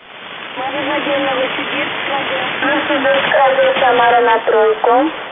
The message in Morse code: none